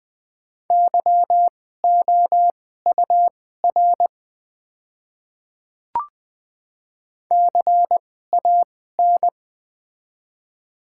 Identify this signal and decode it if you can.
Morse code, heard as a keyed tone, about 20 wpm, spaced as for 15 wpm, 700 Hz. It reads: YOUR CAN
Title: 2x Character Spacing - Top 100 Words - 20wpm